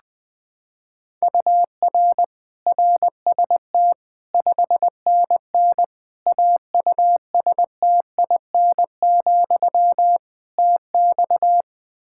Morse code, heard as a keyed tone, about 20 wpm, 700 Hz. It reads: UR RST 5NN AUSTIN, TX
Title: Practice QSO - Exchange 1 - Beginning - 20wpm